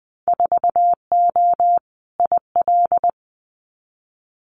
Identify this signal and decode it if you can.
Morse code, heard as a keyed tone, about 20 wpm, 700 Hz. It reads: 4O IL
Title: Winter Field Day Exchanges Encore 20wpm